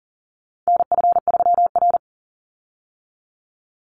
Morse code, heard as a keyed tone, about 37 wpm, 700 Hz. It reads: NF3R
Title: Call Signs - 40wpm